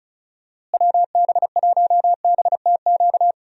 Morse code, heard as a keyed tone, about 34 wpm, 700 Hz. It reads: WB1BTQ
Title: USA Call Signs - 6 Characters - Repeated 3x - 35wpm